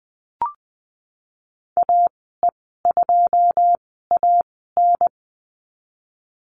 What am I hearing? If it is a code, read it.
Morse code, heard as a keyed tone, about 20 wpm, spaced as for 15 wpm, 700 Hz. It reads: AE2AN
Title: 154 Lesson 25g - Letter p - Call Signs 20wpm